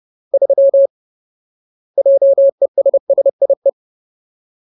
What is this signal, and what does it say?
Morse code, heard as a keyed tone, about 30 wpm, 550 Hz. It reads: 3 JESSIE
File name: Ultimate Word List / 30WPM Ultimate